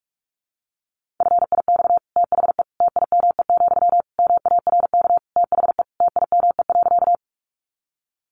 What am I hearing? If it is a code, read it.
Morse code, heard as a keyed tone, about 44 wpm, 700 Hz. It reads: FIX THE TIME, MARK THE TIME.
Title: Sentences from Top 1400 Words 45wpm